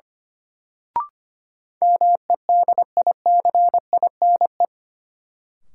Morse code, heard as a keyed tone, about 25 wpm, 700 Hz. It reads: MEDICINE
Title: Sentences from Top 900 Words - New Word Drill - 25wpm